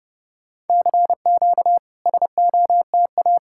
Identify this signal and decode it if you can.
Morse code, heard as a keyed tone, about 30 wpm, 700 Hz. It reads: CQ SOTA
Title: SOTA QSO Practice 30wpm